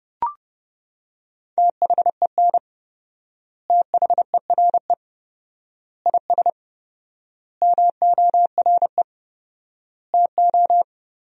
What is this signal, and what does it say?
Morse code, heard as a keyed tone, about 30 wpm, 700 Hz. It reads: THEN THERE IS MORE TO
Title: 4x Word-Spacing - Sentences from Top 100 Words - 30wpm